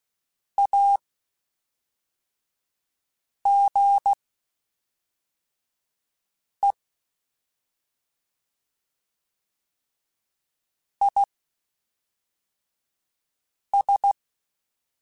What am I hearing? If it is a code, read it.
Morse code, heard as a keyed tone, about 16 wpm, spaced as for 3 wpm, 800 Hz. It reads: AGE IS
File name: Spaced / Chuckles Spaced-mp3s / MiddleAge sp @5